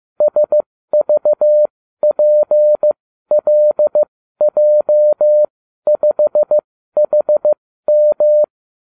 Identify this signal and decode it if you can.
Morse code, heard as a keyed tone, about 15 wpm, spaced as for 12 wpm, 600 Hz. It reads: SVPLJ5HM
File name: cw / ubacw29 2